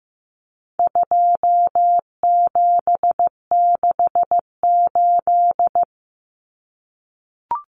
Morse code, heard as a keyed tone, about 15 wpm, 700 Hz. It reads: 2768
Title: Common Cut Numbers - 4-Digit 15wpm